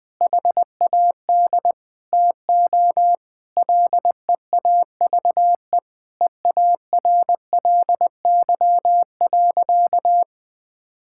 Morse code, heard as a keyed tone, about 20 wpm, 700 Hz. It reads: HAD TO LEAVE EARLY.
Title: Sentences with 2-Word Phrases - 20wpm